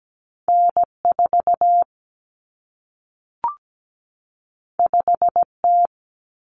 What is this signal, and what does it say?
Morse code, heard as a keyed tone, about 17 wpm, 700 Hz. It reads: N4 5T
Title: Common Cut Numbers - 2-Digit 17wpm